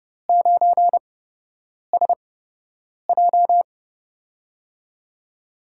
Morse code, heard as a keyed tone, about 30 wpm, spaced as for 8 wpm, 700 Hz. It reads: 9SJ
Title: USA Call Signs - 4 Characters - 8x Character Spacing - 30wpm